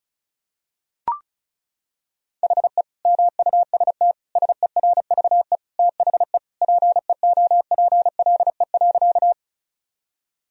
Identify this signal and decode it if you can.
Morse code, heard as a keyed tone, about 36 wpm, 700 Hz. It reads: HE MUST SERVE THE PEOPLE.
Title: Sentences from Top 1000 Words - Review Part 2 - 35wpm